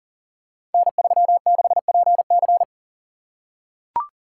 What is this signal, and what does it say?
Morse code, heard as a keyed tone, about 39 wpm, 700 Hz. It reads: N36PC
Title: Code Groups - 5 Letter:Numbers - 40wpm